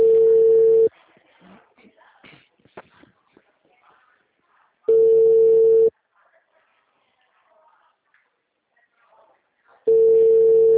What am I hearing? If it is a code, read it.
Ringback tone (China, 450 Hz).